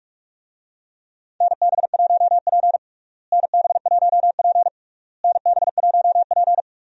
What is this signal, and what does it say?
Morse code, heard as a keyed tone, about 47 wpm, 700 Hz. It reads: NB1P NB1P NB1P
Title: USA Call Signs - 4 Characters - Repeated 3x - 45wpm